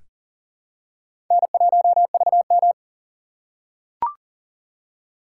Morse code, heard as a keyed tone, about 43 wpm, 690 Hz. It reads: N1VM
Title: USA Call Signs - 4 Characters - 2x Character Spacing - 40wpm